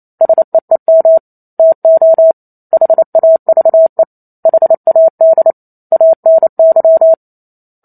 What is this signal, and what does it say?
Morse code, heard as a keyed tone, about 28 wpm, 650 Hz. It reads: SEEM TO HAVE HAD ANY